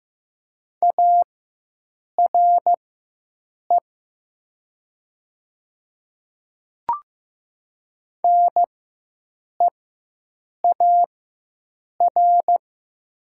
Morse code, heard as a keyed tone, about 15 wpm, spaced as for 7 wpm, 700 Hz. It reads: ARE NEAR
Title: 4x Character Spacing - Top 200 Words - 15wpm